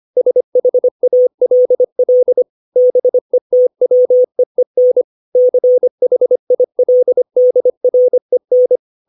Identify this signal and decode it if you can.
Morse code, heard as a keyed tone, about 25 wpm, 500 Hz. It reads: SHALL BETWEEN CHILDREN
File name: 632 words / 632 Words 25WPM